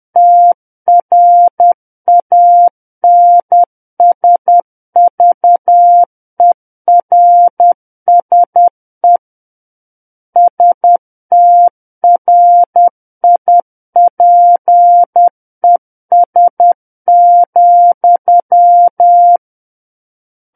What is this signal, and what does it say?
Morse code, heard as a keyed tone, about 10 wpm, 700 Hz. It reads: TRANSVERSE STRIPES,
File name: Book One / 01-Chapter 1 - 10 WPM